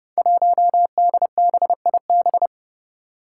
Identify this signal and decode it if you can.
Morse code, heard as a keyed tone, about 32 wpm, 700 Hz. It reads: 1DBIB
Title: Five Challenging Letter-Number Combinations 30wpm